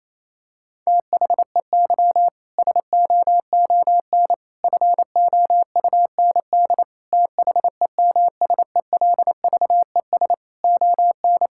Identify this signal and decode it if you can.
Morse code, heard as a keyed tone, about 28 wpm, 700 Hz. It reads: THEY SOON FOUND THEMSELVES ON
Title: Sentences from Top 1000 Words - Review Part 5 - 28wpm